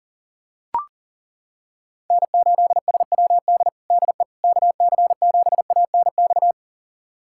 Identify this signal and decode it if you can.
Morse code, heard as a keyed tone, about 40 wpm, 700 Hz. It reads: N8SWD DE KC7ANX
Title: USA Callsign DE Callsign 40wpm